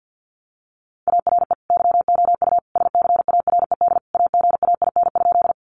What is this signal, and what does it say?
Morse code, heard as a keyed tone, about 50 wpm, 700 Hz. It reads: ARE YOU SCARED AGAIN?
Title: Sentences from Top 700 Words 50wpm